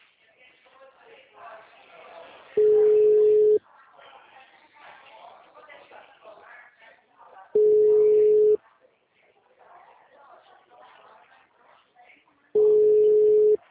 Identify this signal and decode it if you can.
Ringback tone (Europe, 425 Hz), with speech.